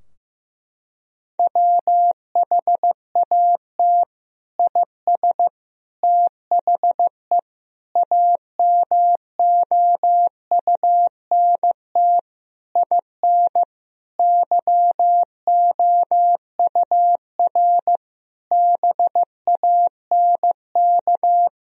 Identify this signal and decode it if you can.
Morse code, heard as a keyed tone, about 15 wpm, 700 Hz. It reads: WHAT IS THE AMOUNT IN YOUR BANK
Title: Sentences from Top 1000 Words - Review Part 4 - 15wpm